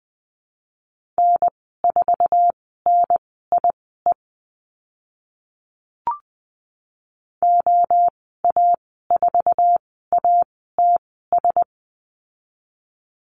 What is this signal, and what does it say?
Morse code, heard as a keyed tone, about 20 wpm, spaced as for 15 wpm, 700 Hz. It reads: N4NIE OA4ATS
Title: 069 Lesson 12g - Letter d - Call Signs 20wpm